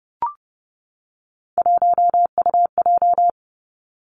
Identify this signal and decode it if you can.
Morse code, heard as a keyed tone, about 30 wpm, 700 Hz. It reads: 1UJ
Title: Three Challenging Letter-Number Combinations 30wpm